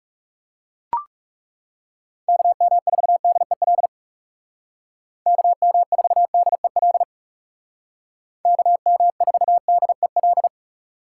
Morse code, heard as a keyed tone, about 36 wpm, 700 Hz. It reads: KM4DEL KM4DEL KM4DEL
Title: Speed Racing - USA 2x3 Call Signs 30wpm